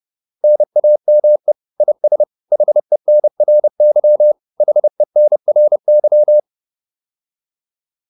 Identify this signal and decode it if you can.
Morse code, heard as a keyed tone, about 30 wpm, 600 Hz. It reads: NAME IS HENRY HENRY